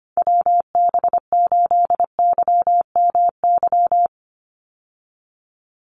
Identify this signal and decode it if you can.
Morse code, heard as a keyed tone, about 25 wpm, 700 Hz. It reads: WB8YMY
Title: USA Call Signs 25wpm